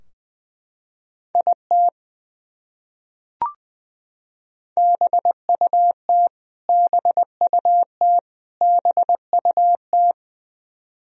Morse code, heard as a keyed tone, about 20 wpm, 700 Hz. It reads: IT BUT BUT BUT
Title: Top 100 Words Repeated 20wpm